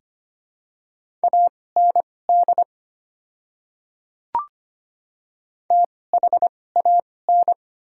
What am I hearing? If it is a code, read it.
Morse code, heard as a keyed tone, about 25 wpm, spaced as for 18 wpm, 700 Hz. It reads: AND THAN
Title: 2x Character Spacing - Top 100 Words - 25wpm